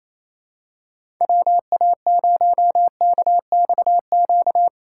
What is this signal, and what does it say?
Morse code, heard as a keyed tone, about 28 wpm, 690 Hz. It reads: WA0KXQ
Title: USA Call Signs - 6 Characters - 4x Character Spacing - 28wpm